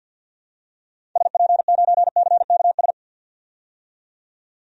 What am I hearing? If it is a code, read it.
Morse code, heard as a keyed tone, about 51 wpm, 700 Hz. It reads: SP9CKS
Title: DXCC-100 Call Signs - 6 Characters - 50wpm